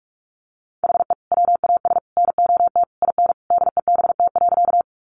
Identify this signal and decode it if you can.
Morse code, heard as a keyed tone, about 46 wpm, 700 Hz. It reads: HE WAS NOT IN DEBT.
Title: Sentences from Top 1300 Words 45wpm